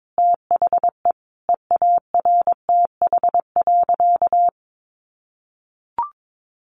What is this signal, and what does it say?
Morse code, heard as a keyed tone, about 22 wpm, 700 Hz. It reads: THE EARTH.
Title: Sentences from Top 1200 Words 22wpm